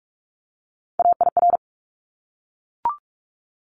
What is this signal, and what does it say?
Morse code, heard as a keyed tone, about 44 wpm, 700 Hz. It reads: AIR
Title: Top 200 Words Repeated 45wpm